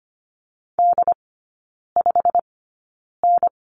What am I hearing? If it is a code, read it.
Morse code, heard as a keyed tone, about 25 wpm, spaced as for 9 wpm, 700 Hz. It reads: D5N
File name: course / 25:10wpm / 093 Lesson 16e - Letter c - Three Characters 25wpm